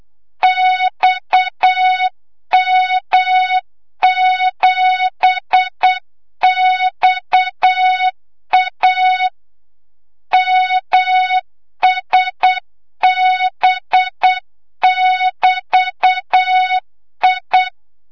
Morse code, heard as a keyed tone, about 8 wpm, 3700 Hz, with noise. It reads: XM7XA MSB<BT>I
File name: Morze-HA2MN / hang / 25 GYAKORLAT